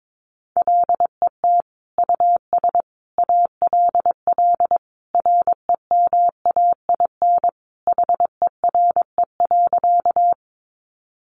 Morse code, heard as a keyed tone, about 22 wpm, 700 Hz. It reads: LET US ALL REMAIN HERE.
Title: Sentences from Top 1200 Words 22wpm